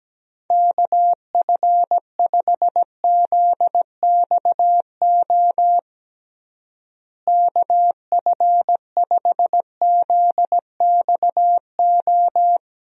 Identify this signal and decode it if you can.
Morse code, heard as a keyed tone, about 17 wpm, 700 Hz. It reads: KF5ZXO KF5ZXO
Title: USA Call Signs - 6 Characters - Repeated 3x - 17wpm